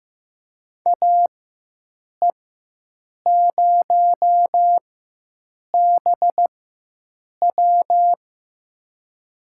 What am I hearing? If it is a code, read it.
Morse code, heard as a keyed tone, about 15 wpm, spaced as for 7 wpm, 700 Hz. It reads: AE0BW
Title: USA Call Signs - 5 Characters - 4x Character Spacing - 15wpm